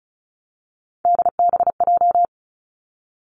Morse code, heard as a keyed tone, about 36 wpm, 700 Hz. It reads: DBJ
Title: Three Challenging Letter-Number Combinations 35wpm